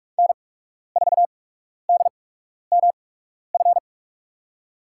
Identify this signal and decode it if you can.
Morse code, heard as a keyed tone, about 44 wpm, spaced as for 12 wpm, 700 Hz. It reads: N4DMF
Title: USA Call Signs - 5 Characters - 8x Character Spacing - 45wpm